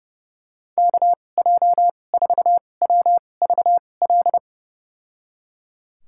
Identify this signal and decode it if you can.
Morse code, heard as a keyed tone, about 32 wpm, spaced as for 22 wpm, 700 Hz. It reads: KJ4WVL
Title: USA Call Signs - 6 Characters - 2x Character Spacing - 30wpm